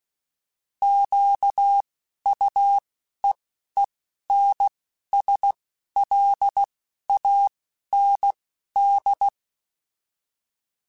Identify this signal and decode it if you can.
Morse code, heard as a keyed tone, about 16 wpm, spaced as for 11 wpm, 800 Hz. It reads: QUEENSLAND